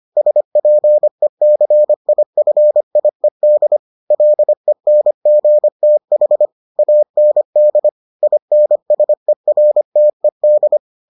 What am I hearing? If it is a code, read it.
Morse code, heard as a keyed tone, about 25 wpm, 600 Hz. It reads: SPECIFIED LENGTH AND INSERTED